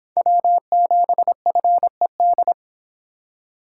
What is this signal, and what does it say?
Morse code, heard as a keyed tone, about 30 wpm, 700 Hz. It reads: W7FED